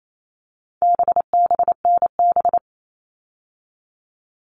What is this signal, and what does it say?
Morse code, heard as a keyed tone, about 28 wpm, 700 Hz. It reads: BBNB
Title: All Cut Numbers - 4-Digit 28wpm